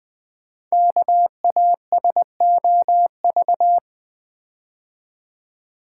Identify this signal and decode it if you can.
Morse code, heard as a keyed tone, about 20 wpm, 700 Hz. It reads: KASOV
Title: Code Groups - 5 Letters - 20wpm